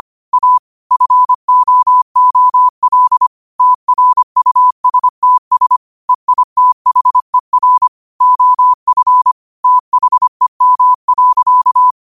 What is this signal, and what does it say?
Morse code, heard as a keyed tone, about 25 wpm, 1000 Hz. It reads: A FOOL TRUSTS EITHER OF THEM.
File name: qotdfiles / 25wpm